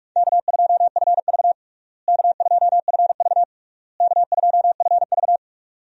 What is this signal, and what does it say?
Morse code, heard as a keyed tone, about 42 wpm, 700 Hz. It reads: K2FV K2FV K2FV
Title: USA Call Signs - 4 Characters - Repeated 3x - 45wpm